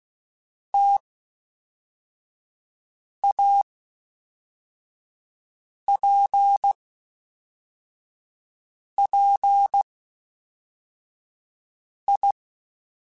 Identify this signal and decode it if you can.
Morse code, heard as a keyed tone, about 16 wpm, 800 Hz. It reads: T A P P I
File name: Difficult Combos / Difficult Combos-mp3s / T AOEI @10